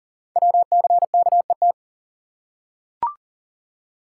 Morse code, heard as a keyed tone, about 40 wpm, 700 Hz. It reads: WCKET
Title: Code Groups - 5 Letters - 40wpm